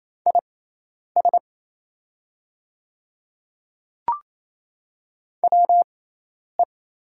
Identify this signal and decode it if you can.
Morse code, heard as a keyed tone, about 27 wpm, spaced as for 10 wpm, 700 Hz. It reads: IS WE
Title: 6x Character Spacing - Top 100 Words - 28wpm